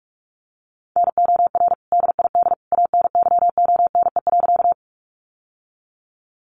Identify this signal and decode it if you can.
Morse code, heard as a keyed tone, about 44 wpm, 700 Hz. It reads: NOR DID ANYONE.
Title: Sentences from Top 1500 Words 45wpm